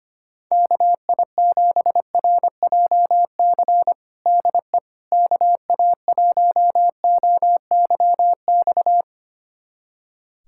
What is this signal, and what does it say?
Morse code, heard as a keyed tone, about 25 wpm, 700 Hz. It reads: KI7RJC DE KA1OYX
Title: USA Callsign DE Callsign - 6 Characters - 25wpm